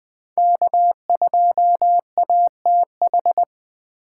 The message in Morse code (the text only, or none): K2ATH